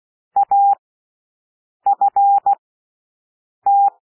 Morse code, heard as a keyed tone, about 16 wpm, spaced as for 6 wpm, 800 Hz. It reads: AFT
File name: cwmp3 / W1 SOUND WORDS 16 wpm